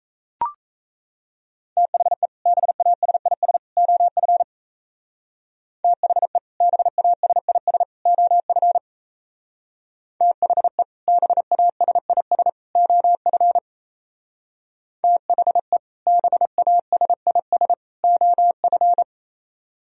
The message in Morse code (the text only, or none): THE BASIS OF THE BASIS OF THE BASIS OF THE BASIS OF